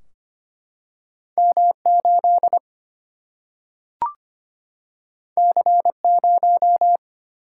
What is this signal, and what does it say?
Morse code, heard as a keyed tone, about 25 wpm, 700 Hz. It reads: M 8 C 0
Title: Single-Letter Single-Number 25wpm